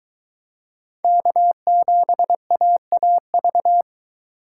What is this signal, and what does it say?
Morse code, heard as a keyed tone, about 23 wpm, 700 Hz. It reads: K7AAV